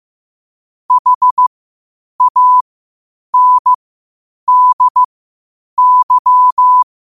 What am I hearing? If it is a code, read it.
Morse code, heard as a keyed tone, about 15 wpm, spaced as for 8 wpm, 1000 Hz. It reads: HANDY
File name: qotdfiles / 8wpm